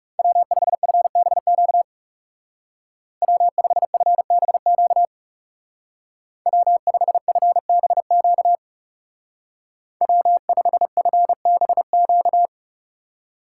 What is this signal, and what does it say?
Morse code, heard as a keyed tone, about 36 wpm, 700 Hz. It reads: W5FBQ W5FBQ W5FBQ W5FBQ
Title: Speed Racing - USA 1x3 Call Signs 30wpm